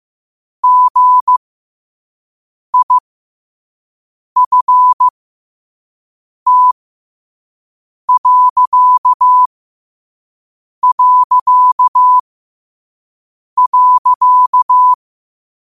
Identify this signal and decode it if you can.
Morse code, heard as a keyed tone, about 15 wpm, spaced as for 5 wpm, 1000 Hz. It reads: GIFT...
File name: qotdfiles / 5wpm